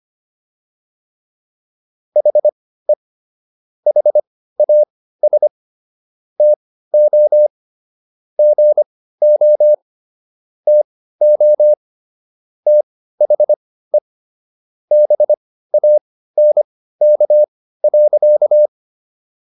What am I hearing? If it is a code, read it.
Morse code, heard as a keyed tone, about 25 wpm, spaced as for 15 wpm, 600 Hz. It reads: HE HAS TO GO TO THE BANK.